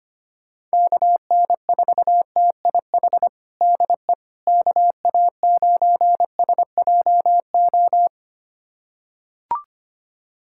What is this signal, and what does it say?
Morse code, heard as a keyed tone, about 25 wpm, 700 Hz. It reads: KN4TIH DE KA9SJO
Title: USA Callsign DE Callsign - 6 Characters - 25wpm